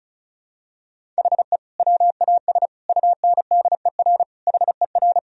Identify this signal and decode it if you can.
Morse code, heard as a keyed tone, about 36 wpm, 700 Hz. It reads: HE WAS UNDER HER
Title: Sentences from Top 1000 Words - Review Part 1 - 35wpm